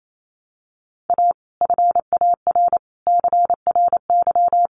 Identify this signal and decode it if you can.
Morse code, heard as a keyed tone, about 28 wpm, 700 Hz. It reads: A FAR CRY
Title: Sets of 5 Words Encore 28wpm